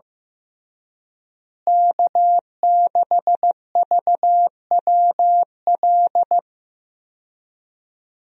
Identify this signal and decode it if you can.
Morse code, heard as a keyed tone, about 15 wpm, 700 Hz. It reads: K6VWL
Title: USA Call Signs - 5 Characters - Repeated 2x - 15wpm